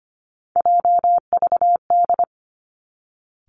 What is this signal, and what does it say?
Morse code, heard as a keyed tone, about 25 wpm, 700 Hz. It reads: JVD
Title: Three Challenging Letter-Number Combinations 25wpm